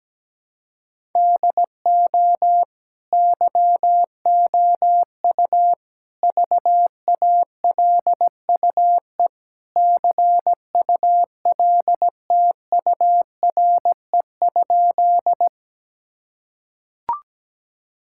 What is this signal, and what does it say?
Morse code, heard as a keyed tone, about 17 wpm, 700 Hz. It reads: DO YOU VALUE CULTURE?
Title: Sentences from Top 1000 Words - Review Part 6 - 17wpm